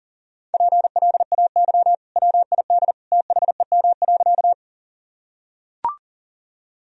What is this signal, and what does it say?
Morse code, heard as a keyed tone, about 40 wpm, 700 Hz. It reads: PLAY WID THEM.
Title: Sentences from Top 200 Words and Abbreviations 40wpm